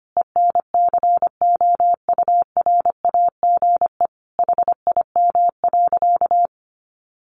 Morse code, heard as a keyed tone, about 25 wpm, 700 Hz. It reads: ENCOURAGE HIM.
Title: Sentences from Top 1400 Words 25wpm